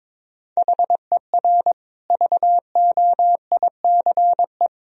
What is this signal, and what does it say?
Morse code, heard as a keyed tone, about 22 wpm, 700 Hz. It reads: HER VOICE
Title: Sentences from Top 1000 Words - Review Part 3 - 22wpm